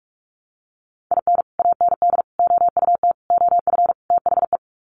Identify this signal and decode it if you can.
Morse code, heard as a keyed tone, about 44 wpm, 700 Hz. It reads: IN AND OUT OF THE
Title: Sets of 5 Words 45wpm